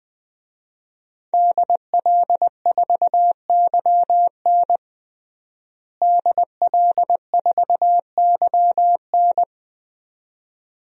Morse code, heard as a keyed tone, about 21 wpm, 700 Hz. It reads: DL4YN DL4YN
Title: DXCC-100 Call Signs - 5 Characters - Repeated 2x - 20wpm